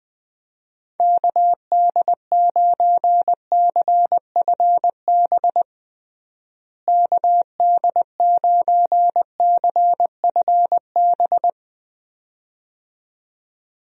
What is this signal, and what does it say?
Morse code, heard as a keyed tone, about 20 wpm, 700 Hz. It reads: KD9CFB KD9CFB
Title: USA Call Signs - 6 Characters - Repeated 2x - 20wpm